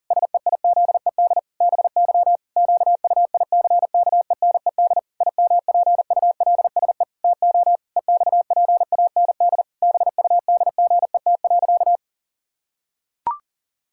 Morse code, heard as a keyed tone, about 40 wpm, 700 Hz. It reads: SEIZED BY QUICKENED IMPULSE TO EXPAND BUDGET.
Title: Pangram Challenge v2 - 40wpm